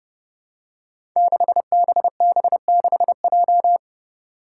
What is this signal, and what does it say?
Morse code, heard as a keyed tone, about 31 wpm, 700 Hz. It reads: 6BB6J
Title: Five Challenging Letter-Number Combinations 30wpm